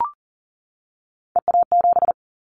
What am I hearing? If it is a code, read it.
Morse code, heard as a keyed tone, about 38 wpm, 700 Hz. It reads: EA7
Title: Callsign Prefixes for Top 25 DXCC - 40wpm